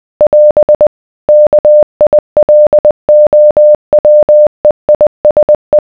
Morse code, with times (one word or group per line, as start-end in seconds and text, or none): 0.2-0.9 <AS>
1.3-5.8 KILOWEISE